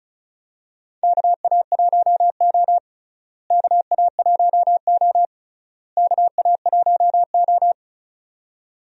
Morse code, heard as a keyed tone, about 34 wpm, 700 Hz. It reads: KA1O KA1O KA1O
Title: USA Call Signs - 4 Characters - Repeated 3x - 35wpm